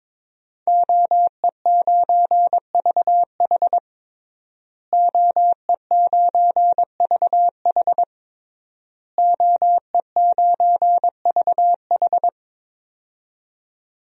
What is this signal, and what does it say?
Morse code, heard as a keyed tone, about 23 wpm, 700 Hz. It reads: OE9VH OE9VH OE9VH
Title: DXCC-100 Call Signs - 5 Characters - Repeated 3x - 22wpm